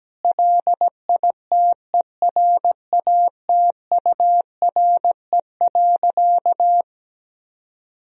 Morse code, heard as a keyed tone, about 17 wpm, 700 Hz. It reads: LITERATURE.
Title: Sentences from Top 1000 Words - Review Part 3 - 17wpm